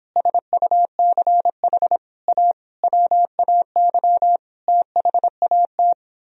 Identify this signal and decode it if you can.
Morse code, heard as a keyed tone, about 26 wpm, 700 Hz. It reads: SUCH A WAY THAT